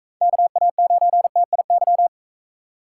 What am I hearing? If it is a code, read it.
Morse code, heard as a keyed tone, about 45 wpm, 700 Hz. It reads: KA9TIY